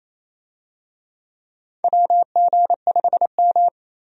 Morse code, heard as a keyed tone, about 28 wpm, 700 Hz. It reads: WG5M
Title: USA Call Signs - 4 Characters - 28wpm